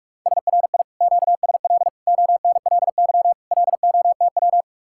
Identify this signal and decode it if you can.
Morse code, heard as a keyed tone, about 50 wpm, 700 Hz. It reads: SRI QSL ONLY LOTW